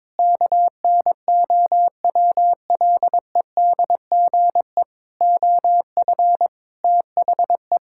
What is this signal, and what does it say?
Morse code, heard as a keyed tone, about 22 wpm, 700 Hz. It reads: KNOWLEDGE OF THE
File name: Sets of 9 Words - 22wpm